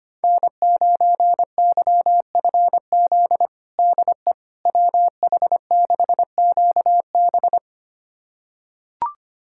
Morse code, heard as a keyed tone, about 25 wpm, 700 Hz. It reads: N9YFZ DE WH6QB
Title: USA Callsign DE Callsign - 5 Characters - 25wpm